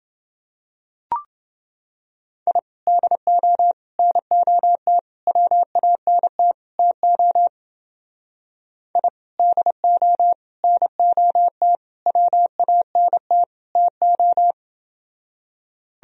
Morse code, heard as a keyed tone, about 28 wpm, 700 Hz. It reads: I DO NOT WANT TO I DO NOT WANT TO